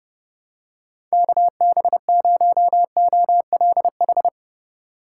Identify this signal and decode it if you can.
Morse code, heard as a keyed tone, about 30 wpm, 700 Hz. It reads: KB0OLH
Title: USA Call Signs - 6 Characters - 4x Character Spacing - 30wpm